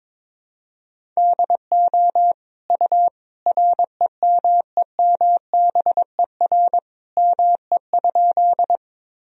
Morse code, heard as a keyed tone, about 22 wpm, 700 Hz. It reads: DO U REMEMBER ME?
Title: Sentences from Top 400 Words and Abbreviations 22wpm